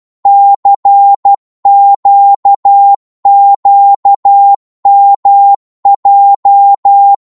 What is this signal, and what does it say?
Morse code, heard as a keyed tone, about 12 wpm, 800 Hz. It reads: CQQMJ
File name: Morse / ltr-12wpm